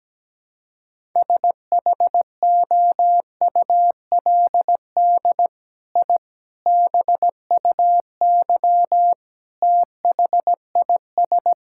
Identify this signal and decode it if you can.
Morse code, heard as a keyed tone, about 17 wpm, 700 Hz. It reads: SHOULD I BUY THIS
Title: Sentences from Top 1000 Words - Review Part 5 - 17wpm